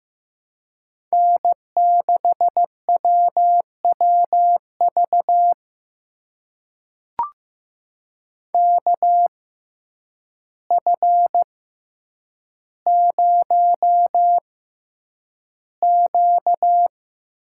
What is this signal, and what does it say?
Morse code, heard as a keyed tone, about 15 wpm, spaced as for 8 wpm, 700 Hz. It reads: N6WWV KF0Q
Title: USA Call Signs - 5 Characters - 6x Character Spacing - 15wpm